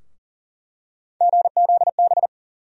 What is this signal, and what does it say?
Morse code, heard as a keyed tone, about 40 wpm, 700 Hz. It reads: GZB
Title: USA Call Sign Postfixes - 3 Letters 40wpm